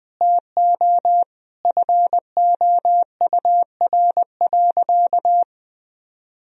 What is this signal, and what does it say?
Morse code, heard as a keyed tone, about 20 wpm, 700 Hz. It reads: TO FOUR.
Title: Sentences from Top 1000 Words - Review Part 1 - 20wpm